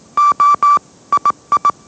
Morse code, heard as a keyed tone, about 18 wpm, 1160 Hz. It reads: O II
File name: mrx / mrx082 20wpm